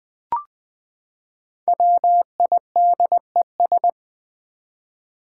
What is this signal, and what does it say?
Morse code, heard as a keyed tone, about 20 wpm, 700 Hz. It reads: WIDES
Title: Build A Word - 5 Letters At A Time - 10-Letter Words - 20wpm